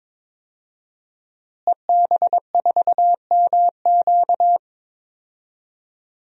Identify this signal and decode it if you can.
Morse code, heard as a keyed tone, about 22 wpm, 700 Hz. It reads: EB4MQ
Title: DXCC-100 Call Signs - 5 Characters - Repeated 3x - 22wpm